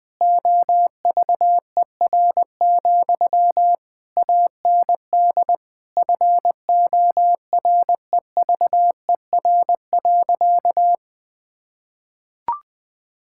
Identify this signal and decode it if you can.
Morse code, heard as a keyed tone, about 20 wpm, 700 Hz. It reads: OVER, AND FOREVER.
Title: Sentences from Top 1600 Words 20wpm